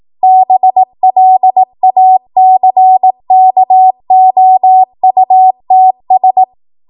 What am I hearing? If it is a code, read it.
Morse code, heard as a keyed tone, about 18 wpm, 750 Hz. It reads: BLACKOUTS
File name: Morse / 18WPM